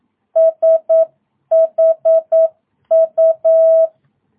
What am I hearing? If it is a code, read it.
Morse code, heard as a keyed tone, about 9 wpm, 660 Hz. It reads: SHU
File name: jh8cbh / 2022cw